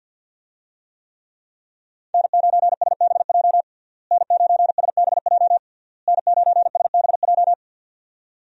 Morse code, heard as a keyed tone, about 52 wpm, 700 Hz. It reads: N9SBJ N9SBJ N9SBJ
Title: USA Call Signs - 5 Characters - Repeated 3x - 50wpm